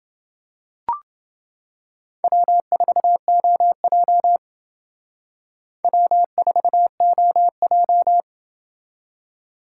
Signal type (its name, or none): Morse code